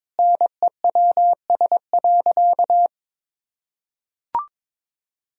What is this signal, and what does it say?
Morse code, heard as a keyed tone, about 21 wpm, 700 Hz. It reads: NEWS.
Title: Sentences from Top 1000 Words - Review Part 4 - 22wpm